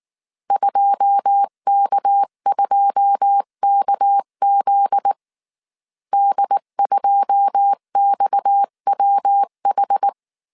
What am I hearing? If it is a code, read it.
Morse code, heard as a keyed tone, about 19 wpm, 800 Hz. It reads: 2K2KZ D2XWH
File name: CW / DET / 1 9 1500C S C 18wpm 2539